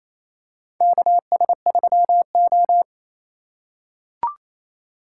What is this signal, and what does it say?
Morse code, heard as a keyed tone, about 28 wpm, 700 Hz. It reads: KS3O
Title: USA Call Signs - 2x Character Spacing - 28wpm